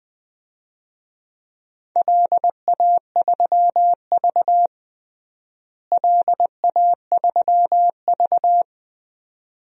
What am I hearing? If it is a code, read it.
Morse code, heard as a keyed tone, about 20 wpm, 700 Hz. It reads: LA3V LA3V
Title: DXCC-100 Call Signs - 4 Characters - Repeated 2x - 20wpm